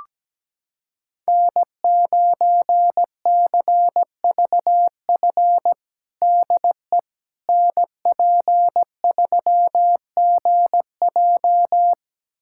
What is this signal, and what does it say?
Morse code, heard as a keyed tone, about 17 wpm, 700 Hz. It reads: N9CVF DE NP3GJ
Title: USA Callsign DE Callsign - 5 Characters - 17wpm